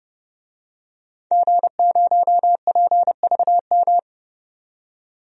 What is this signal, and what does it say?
Morse code, heard as a keyed tone, about 31 wpm, 700 Hz. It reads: G0PVM
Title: DXCC-100 Call Signs - 5 Characters - 30wpm